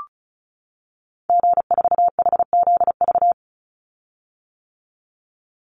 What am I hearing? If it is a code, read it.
Morse code, heard as a keyed tone, about 34 wpm, 700 Hz. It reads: G4HZV
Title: Call Signs - 35wpm